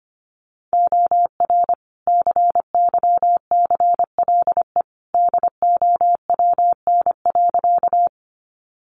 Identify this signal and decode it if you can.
Morse code, heard as a keyed tone, about 25 wpm, 700 Hz. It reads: OR CYCLE DOWN.
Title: Sentences from Top 900 Words 25wpm